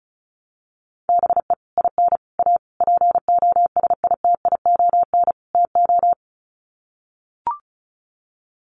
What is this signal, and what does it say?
Morse code, heard as a keyed tone, about 34 wpm, 700 Hz. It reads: BE IN A POSITION TO
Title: Sets of 5 Words Encore2 35wpm